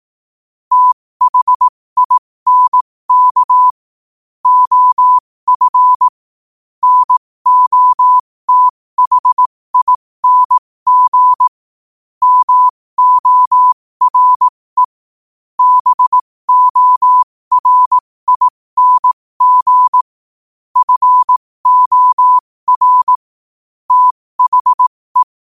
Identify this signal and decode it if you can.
Morse code, heard as a keyed tone, about 18 wpm, spaced as for 15 wpm, 1000 Hz. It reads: THINK OF NOTHING MORE BORING FOR THE
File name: qotdfiles / 15wpm